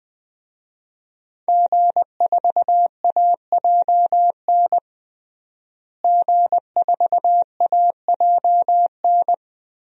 Morse code, heard as a keyed tone, about 20 wpm, 700 Hz. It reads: G4AJN G4AJN
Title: DXCC-100 Call Signs - 5 Characters - Repeated 2x - 20wpm